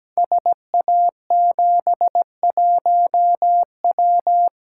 Morse code, heard as a keyed tone, about 17 wpm, 700 Hz. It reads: SA71W
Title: Code Groups - 5 Letter:Numbers - 17wpm